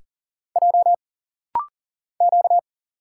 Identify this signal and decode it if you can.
Morse code, heard as a keyed tone, about 40 wpm, 700 Hz. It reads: J Q
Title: Single Letters - Rapid-Fire 40wpm